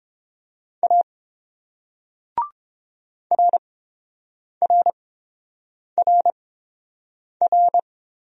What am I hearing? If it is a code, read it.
Morse code, heard as a keyed tone, about 30 wpm, spaced as for 7 wpm, 700 Hz. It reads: A RRRR